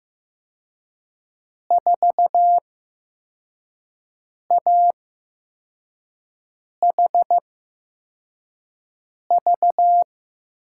Morse code, heard as a keyed tone, about 15 wpm, spaced as for 4 wpm, 700 Hz. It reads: 4AHV
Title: USA Call Signs - 6 Characters - 8x Character Spacing - 15wpm